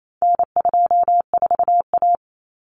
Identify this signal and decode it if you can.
Morse code, heard as a keyed tone, about 28 wpm, 700 Hz. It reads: N24A
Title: Common Cut Numbers - 4-Digit 28wpm